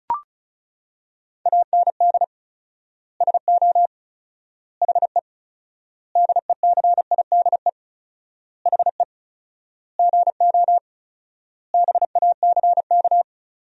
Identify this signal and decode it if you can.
Morse code, heard as a keyed tone, about 36 wpm, 700 Hz. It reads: AND SO HE DECIDE HE GO BACK
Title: 4x Word-Spacing - Sentences from Top 500 Words - 35wpm